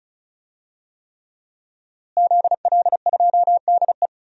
Morse code, heard as a keyed tone, about 36 wpm, 700 Hz. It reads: ZL2DE
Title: DXCC-100 Call Signs - 5 Characters - Repeated 3x - 35wpm